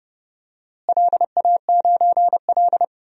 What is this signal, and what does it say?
Morse code, heard as a keyed tone, about 30 wpm, 700 Hz. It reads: LA9L
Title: DXCC-100 Call Signs - 4 Characters - Repeated 3x - 30wpm